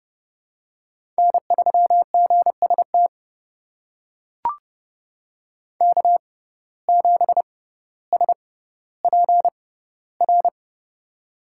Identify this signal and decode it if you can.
Morse code, heard as a keyed tone, about 30 wpm, spaced as for 15 wpm, 700 Hz. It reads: N3GST K7SPR
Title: USA Call Signs - 5 Characters - 6x Character Spacing - 30wpm